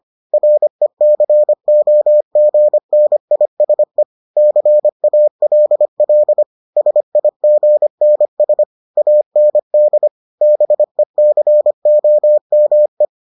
Morse code, heard as a keyed tone, about 25 wpm, 600 Hz. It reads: RECOGNISE CALL SIGNS AND BECOME